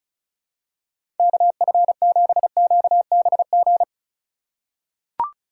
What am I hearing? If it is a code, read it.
Morse code, heard as a keyed tone, about 36 wpm, 700 Hz. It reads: KF7QBG
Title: USA Call Signs - 6 Characters - 4x Character Spacing - 35wpm